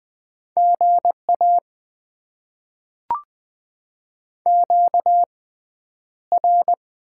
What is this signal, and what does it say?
Morse code, heard as a keyed tone, about 20 wpm, spaced as for 10 wpm, 700 Hz. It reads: GA QR
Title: Top 20 QSO Elements - 6x Character Spacing - 20wpm